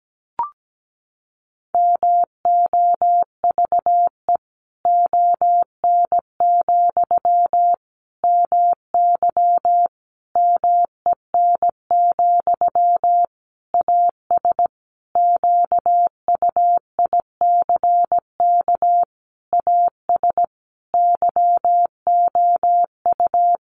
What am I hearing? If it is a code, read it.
Morse code, heard as a keyed tone, about 17 wpm, 700 Hz. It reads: MOVE ON, MY MEN, AS QUICK AS YOU
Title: Sentences from Top 500 Words 17wpm